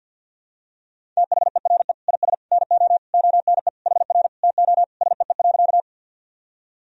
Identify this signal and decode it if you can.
Morse code, heard as a keyed tone, about 46 wpm, 700 Hz. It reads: THERE IS NO ONE HR TO SEE.
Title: Sentences from Top 200 Words and Abbreviations 50wpm